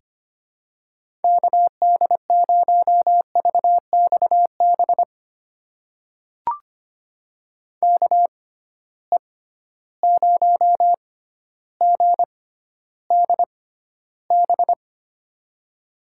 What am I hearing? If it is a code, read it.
Morse code, heard as a keyed tone, about 25 wpm, spaced as for 13 wpm, 700 Hz. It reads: KD0VXB KE0GDB
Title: USA Call Signs - 6 Characters - 6x Character Spacing - 25wpm